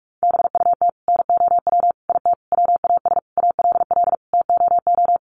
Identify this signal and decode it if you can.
Morse code, heard as a keyed tone, about 44 wpm, 700 Hz. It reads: BUT NOW IT WAS ALL TOO
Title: Sentences from Top 700 Words 45wpm